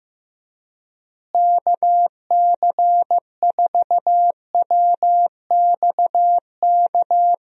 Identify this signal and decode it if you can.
Morse code, heard as a keyed tone, about 15 wpm, 700 Hz. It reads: KC4WXK
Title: USA Call Signs - 6 Characters - Repeated 3x - 15wpm